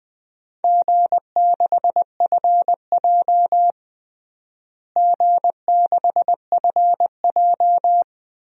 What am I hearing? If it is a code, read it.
Morse code, heard as a keyed tone, about 21 wpm, 700 Hz. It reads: G6FJ G6FJ
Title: DXCC-100 Call Signs - 4 Characters - Repeated 3x - 20wpm